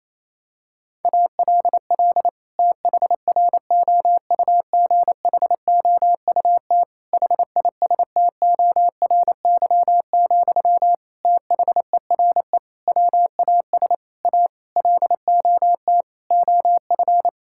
Morse code, heard as a keyed tone, about 28 wpm, 700 Hz. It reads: ALL THROUGHOUT HISTORY, THERE WAS A LOT OF
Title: Sentences from Top 1000 Words - Review Part 5 - 28wpm